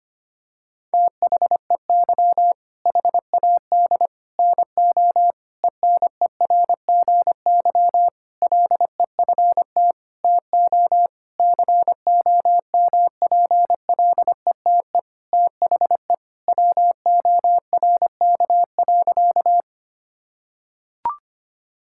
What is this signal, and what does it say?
Morse code, heard as a keyed tone, about 25 wpm, 700 Hz. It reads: THEY HAD NO ENERGY LEFT TO COMPLETE THE WORK.
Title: Sentences from Top 1000 Words - Review Part 6 - 25wpm